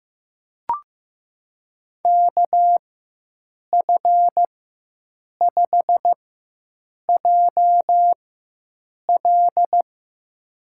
Morse code, heard as a keyed tone, about 15 wpm, spaced as for 7 wpm, 700 Hz. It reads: KF5JL
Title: USA Call Signs - 5 Characters - 4x Character Spacing - 15wpm